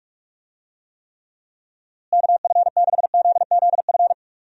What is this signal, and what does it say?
Morse code, heard as a keyed tone, about 43 wpm, 700 Hz. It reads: KF6ZZF
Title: USA Call Signs - 6 Characters - Repeated 3x - 45wpm